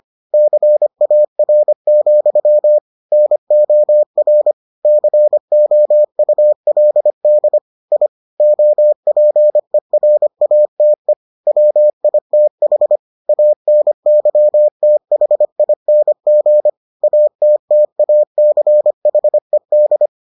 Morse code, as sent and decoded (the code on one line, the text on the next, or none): -.-. .- .-. --..-- / -. --- .-. / -.-. --- ..- .-.. -.. / .. / --- .--. .-. .- - . / .-- .. - .... / .- -. -.-- - .... .. -. --. / .- - - .- -.-. .... -..
CAR, NOR COULD I OPERATE WITH ANYTHING ATTACHED